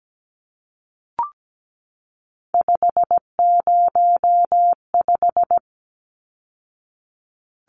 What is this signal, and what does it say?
Morse code, heard as a keyed tone, about 17 wpm, 700 Hz. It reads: 505
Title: Numbers - 3-Digit 17wpm